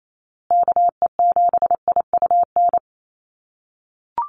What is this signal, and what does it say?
Morse code, heard as a keyed tone, about 28 wpm, 700 Hz. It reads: KE7IUN
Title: USA Call Signs Repeated 3x 28wpm